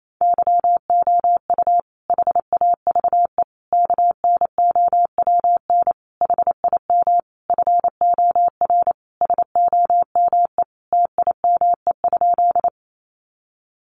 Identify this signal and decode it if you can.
Morse code, heard as a keyed tone, about 28 wpm, 700 Hz. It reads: YOU HAVE KNOWN HIM FOR SOME TIME?
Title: Sentences from Top 600 Words 28wpm